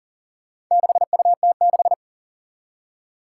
Morse code, heard as a keyed tone, about 40 wpm, 700 Hz. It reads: 6UT6
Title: Four Challenging Letter-Number Combinations 40wpm